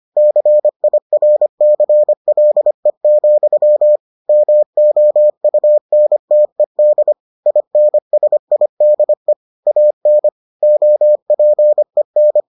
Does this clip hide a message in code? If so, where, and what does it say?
Yes: Morse code from 0.2 to 12.4 s, heard as a keyed tone, about 25 wpm, 600 Hz. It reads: CIRCLE, MOUNTED INSIDE AN OPEN